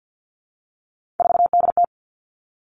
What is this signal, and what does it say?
Morse code, heard as a keyed tone, about 50 wpm, 700 Hz. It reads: 4DT
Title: All Cut Numbers - 3-Digit 50wpm